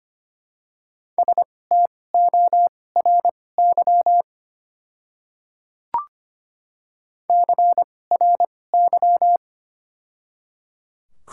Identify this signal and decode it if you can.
Morse code, heard as a keyed tone, about 25 wpm, spaced as for 18 wpm, 700 Hz. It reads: STORY CRY
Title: 2x Character Spacing - Top 500 Words - 25wpm